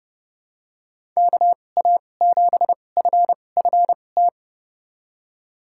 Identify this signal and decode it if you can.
Morse code, heard as a keyed tone, about 30 wpm, spaced as for 22 wpm, 700 Hz. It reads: KA7FFT
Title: USA Call Signs - 6 Characters - 2x Character Spacing - 30wpm